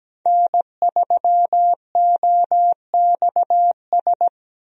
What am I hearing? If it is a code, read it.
Morse code, heard as a keyed tone, about 17 wpm, 700 Hz. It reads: N3OXS
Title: USA Call Signs - 5 Characters - Repeated 2x - 17wpm